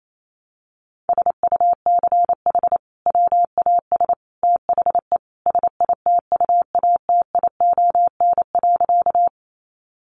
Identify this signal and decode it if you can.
Morse code, heard as a keyed tone, about 28 wpm, 700 Hz. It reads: SUCH WAS THE SITUATION.
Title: Sentences from Top 1200 Words 28wpm